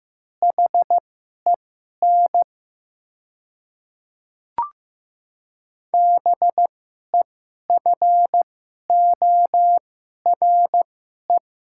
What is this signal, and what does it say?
Morse code, heard as a keyed tone, about 16 wpm, spaced as for 11 wpm, 700 Hz. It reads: HEN BEFORE
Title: 2x Character Spacing - Top 200 Words - 15wpm